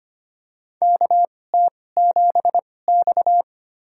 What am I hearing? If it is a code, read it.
Morse code, heard as a keyed tone, about 25 wpm, spaced as for 18 wpm, 700 Hz. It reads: KT7X
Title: USA Call Signs - 4 Characters - 2x Character Spacing - 25wpm